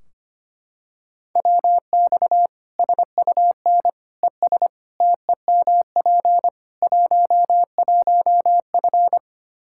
Morse code, heard as a keyed tone, about 25 wpm, 700 Hz. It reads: WX SUN ES TEMP 11F
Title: Practice QSO - Exchange 2 - Info - 25wpm